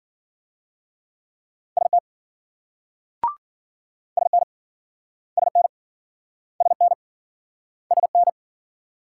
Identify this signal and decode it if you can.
Morse code, heard as a keyed tone, about 53 wpm, 700 Hz. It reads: ST SN SN SN SN